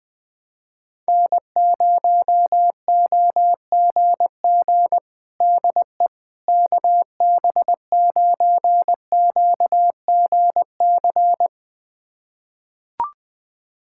Morse code, heard as a keyed tone, about 21 wpm, 700 Hz. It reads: N0OGG DE KB9QGC
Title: USA Callsign DE Callsign 20wpm